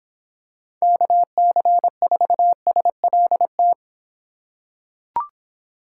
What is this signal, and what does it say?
Morse code, heard as a keyed tone, about 25 wpm, 700 Hz. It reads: KC4SLT